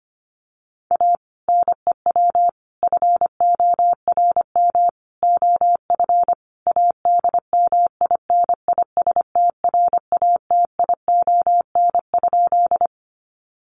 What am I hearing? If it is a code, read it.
Morse code, heard as a keyed tone, about 25 wpm, 700 Hz. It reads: A NEW FORM OF ADMINISTRATION?
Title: Sentences from Top 1300 Words 25wpm